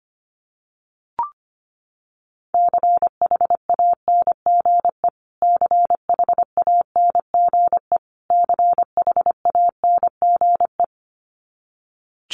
Morse code, heard as a keyed tone, about 25 wpm, 700 Hz. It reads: CHANGE CHANGE CHANGE
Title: Top 300 Words Repeated 25wpm